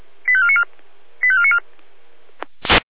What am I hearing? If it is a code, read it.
Five-tone selective call (ZVEI), heard as quick runs of five tones, tones of about 70 ms. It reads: 85494 85494
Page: SDR/ZVEI